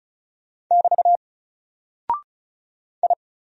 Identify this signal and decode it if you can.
Morse code, heard as a keyed tone, about 36 wpm, 700 Hz. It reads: <BT> I
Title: Single Character (Letter:Number:Punctuation:Prosign) 35wpm